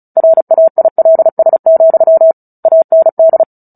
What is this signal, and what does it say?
Morse code, heard as a keyed tone, about 34 wpm, 650 Hz. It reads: RAILS, AND
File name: Book Two / 09-Chapter 9 - 35 WPM